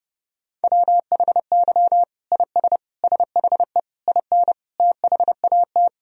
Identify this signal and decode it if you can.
Morse code, heard as a keyed tone, about 30 wpm, 700 Hz. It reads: WHY IS SHE IN THAT
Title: Sentences from Top 1000 Words - Review Part 2 - 30wpm